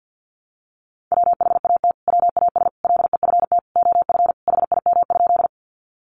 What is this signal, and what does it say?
Morse code, heard as a keyed tone, about 50 wpm, 700 Hz. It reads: WHAT WAS LEFT OF HIM?
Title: Sentences from Top 300 Words 50wpm